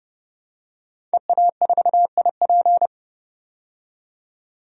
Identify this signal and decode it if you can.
Morse code, heard as a keyed tone, about 30 wpm, 700 Hz. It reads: EA4IP
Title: DXCC-100 Call Signs - 5 Characters - 30wpm